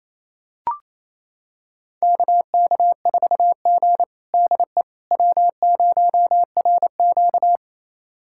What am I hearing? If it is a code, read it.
Morse code, heard as a keyed tone, about 28 wpm, 700 Hz. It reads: KK4G DE W0RQ
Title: USA Callsign DE Callsign - 4 Characters - 28wpm